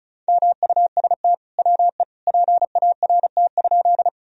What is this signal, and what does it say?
Morse code, heard as a keyed tone, about 36 wpm, 700 Hz. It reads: MUST WE PART?
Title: Sentences from Top 1000 Words - Review Part 1 - 35wpm